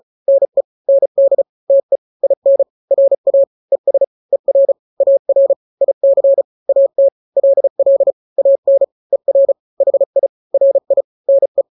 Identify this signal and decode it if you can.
Morse code, heard as a keyed tone, about 34 wpm, 550 Hz. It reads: NE ND TE IN RA ES ER AR IC AT LL AN ER HI RI NE
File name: n-grams / 2-grams 35 30 WPM